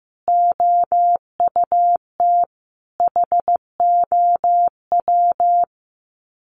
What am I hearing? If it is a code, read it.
Morse code, heard as a keyed tone, about 15 wpm, 700 Hz. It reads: OUT HOW
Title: Sets of 5 Words Encore 15wpm